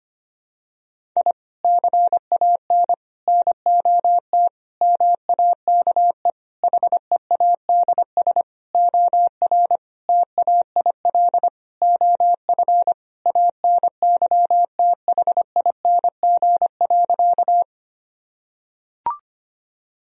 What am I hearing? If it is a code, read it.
Morse code, heard as a keyed tone, about 25 wpm, 700 Hz. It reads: I CAN NOT MAKE HEADS OR TAIL OF ANYTHING.
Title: Sentences from Top 1000 Words - Review Part 3 - 25wpm